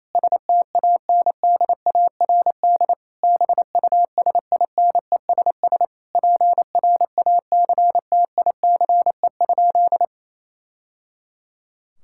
Morse code, heard as a keyed tone, about 28 wpm, 700 Hz. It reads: STANDARD BUSINESS PRACTICE?
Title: Sentences from Top 1000 Words - Review Part 6 - 28wpm